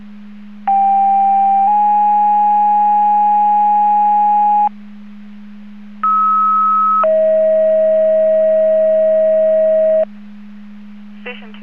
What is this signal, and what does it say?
Two-tone page, heard as a one-second tone, then a three-second tone, about 790 then 830 Hz, about 1290 then 640 Hz. It is heard from 0.7 to 10.0 s.